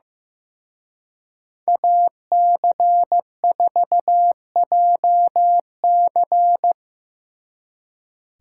Morse code, heard as a keyed tone, about 17 wpm, 700 Hz. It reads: AC4JC
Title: Speed Racing - USA Call Signs 15wpm